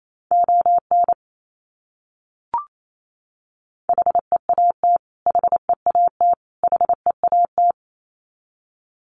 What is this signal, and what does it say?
Morse code, heard as a keyed tone, about 28 wpm, 700 Hz. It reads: ON HEAT HEAT HEAT
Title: Top 500 Words Repeated 28wpm